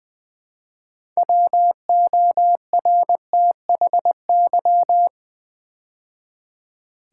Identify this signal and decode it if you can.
Morse code, heard as a keyed tone, about 20 wpm, 700 Hz. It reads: WORTHY
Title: Build A Word - 3 Letters At A Time - 6-Letter Words - 20wpm